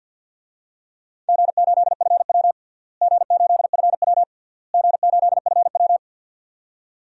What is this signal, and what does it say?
Morse code, heard as a keyed tone, about 50 wpm, 700 Hz. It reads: G8FW G8FW G8FW
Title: DXCC-100 Call Signs - 4 Characters - Repeated 3x - 50wpm